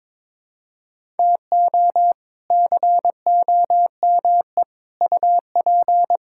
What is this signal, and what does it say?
Morse code, heard as a keyed tone, about 22 wpm, 700 Hz. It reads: TO COME UP
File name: practice / 22wpm / Sets of 7 Words - 22wpm